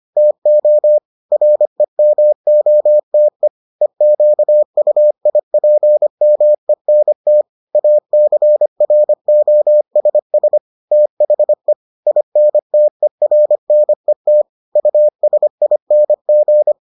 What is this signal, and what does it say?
Morse code, heard as a keyed tone, about 25 wpm, 600 Hz. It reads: TO REMOTE EQUIPMENT ACROSS THE INTERNET USING